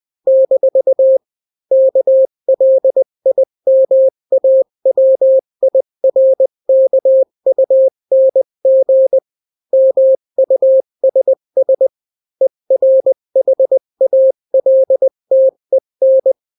Morse code, heard as a keyed tone, about 20 wpm, 530 Hz. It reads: - KLIMAWIRKUNG MUSS ERHALTEN
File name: daily / 18wpm